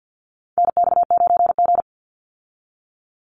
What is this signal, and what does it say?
Morse code, heard as a keyed tone, about 50 wpm, 700 Hz. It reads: NX9G
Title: Call Signs Repeated 3x - 50wpm